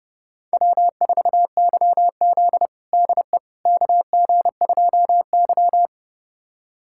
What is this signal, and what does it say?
Morse code, heard as a keyed tone, about 30 wpm, 700 Hz. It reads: W4YZ DE KG2Y
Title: USA Callsign DE Callsign - 4 Characters - 30wpm